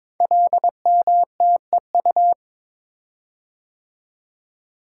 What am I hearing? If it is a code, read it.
Morse code, heard as a keyed tone, about 22 wpm, 700 Hz. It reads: LMTEU
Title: Code Groups - 5 Letters - 22wpm